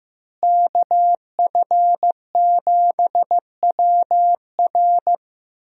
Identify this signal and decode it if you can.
Morse code, heard as a keyed tone, about 15 wpm, 700 Hz. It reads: KF7WR
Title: USA Call Signs - 5 Characters - Repeated 3x - 15wpm